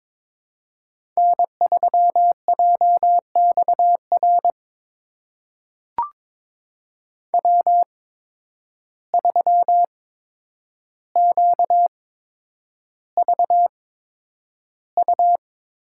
Morse code, heard as a keyed tone, about 22 wpm, spaced as for 9 wpm, 700 Hz. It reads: N3JXR W3QVU
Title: USA Call Signs - 5 Characters - 8x Character Spacing - 22wpm